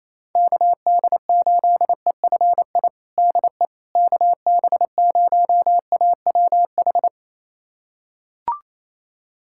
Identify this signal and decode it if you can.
Morse code, heard as a keyed tone, about 28 wpm, 700 Hz. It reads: KD8EFI DE KB0AWH
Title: USA Callsign DE Callsign - 6 Characters - 28wpm